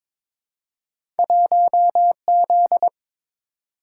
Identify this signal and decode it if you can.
Morse code, heard as a keyed tone, about 21 wpm, 700 Hz. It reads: 1Z
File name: practice / 22wpm / Two Letter-Number Combinations - 22wpm